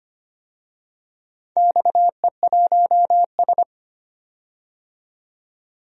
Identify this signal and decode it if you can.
Morse code, heard as a keyed tone, about 25 wpm, 700 Hz. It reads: XE1S
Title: DXCC-100 Call Signs - 4 Characters - 25wpm